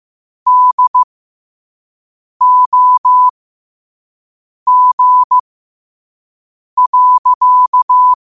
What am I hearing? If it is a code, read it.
Morse code, heard as a keyed tone, about 15 wpm, spaced as for 5 wpm, 1000 Hz. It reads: DOG.
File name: qotdfiles / 5wpm